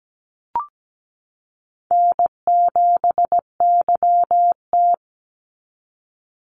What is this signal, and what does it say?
Morse code, heard as a keyed tone, about 17 wpm, 700 Hz. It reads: N7YT
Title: Call Signs Repeated 3x - 17wpm